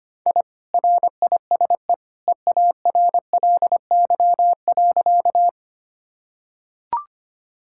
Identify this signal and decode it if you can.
Morse code, heard as a keyed tone, about 25 wpm, 700 Hz. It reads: I RISE EARLY.
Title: Sentences from Top 1000 Words - Review Part 6 - 25wpm